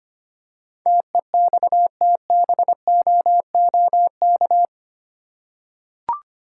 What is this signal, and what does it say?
Morse code, heard as a keyed tone, about 25 wpm, 700 Hz. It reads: TEXTBOOK
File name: 8 Letter Words - First 3 Letters Given - 25wpm